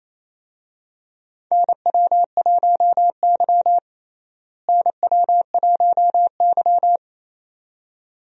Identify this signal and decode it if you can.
Morse code, heard as a keyed tone, about 28 wpm, 700 Hz. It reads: NW1Y NW1Y
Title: USA Call Signs - 4 Characters - Repeated 2x - 28wpm